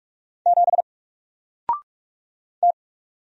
Morse code, heard as a keyed tone, about 44 wpm, 700 Hz. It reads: 7 T
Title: Single Letter-Number 45wpm